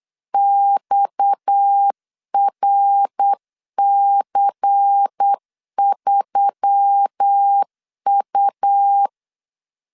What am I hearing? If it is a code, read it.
Morse code, heard as a keyed tone, about 8 wpm, 800 Hz. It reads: XRC3U
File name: LEZIONI / 6 PKX0 / 6 PKX 8wpm 3041